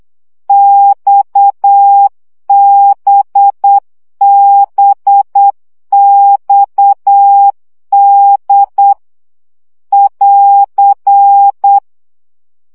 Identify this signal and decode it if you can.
Morse code, heard as a keyed tone, about 8 wpm, 800 Hz. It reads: XBBXD <AR>
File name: CW gyakorlas d b x 7 baud